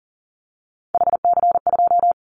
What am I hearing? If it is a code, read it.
Morse code, heard as a keyed tone, about 40 wpm, 700 Hz. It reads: HC2
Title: Callsign Prefixes for Top 100 DXCC - 40wpm